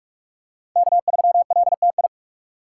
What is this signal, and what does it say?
Morse code, heard as a keyed tone, about 46 wpm, 700 Hz. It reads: K3LTI
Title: USA Call Signs - 5 Characters - 3x Character Spacing - 45wpm